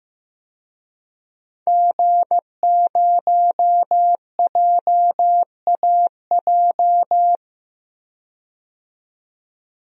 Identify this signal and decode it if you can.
Morse code, heard as a keyed tone, about 15 wpm, 700 Hz. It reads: G0JAJ
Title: DXCC-100 Call Signs - 5 Characters - 15wpm